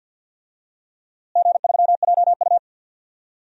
Morse code, heard as a keyed tone, about 47 wpm, 700 Hz. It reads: G3JU
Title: DXCC-100 Call Signs - 4 Characters - Repeated 2x - 50wpm